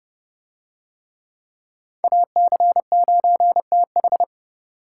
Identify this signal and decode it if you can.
Morse code, heard as a keyed tone, about 27 wpm, 700 Hz. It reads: AC9TH